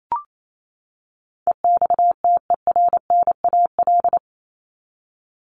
Morse code, heard as a keyed tone, about 28 wpm, 700 Hz. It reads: EXTERNAL
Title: Top 1751-2000 Words 28wpm